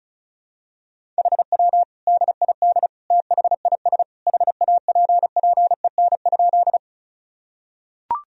Morse code, heard as a keyed tone, about 36 wpm, 700 Hz. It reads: HW DID THIS HAPPEN?
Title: Sentences from Top 400 Words and Abbreviations 35wpm